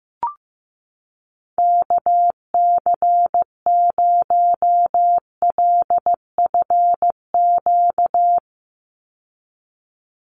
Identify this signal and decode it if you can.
Morse code, heard as a keyed tone, about 15 wpm, 700 Hz. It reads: KC0LFQ
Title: USA Call Signs 15wpm